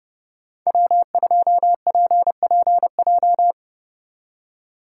Morse code, heard as a keyed tone, about 30 wpm, 700 Hz. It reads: W2PPJ
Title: USA Call Signs - 4x Character Spacing - 30wpm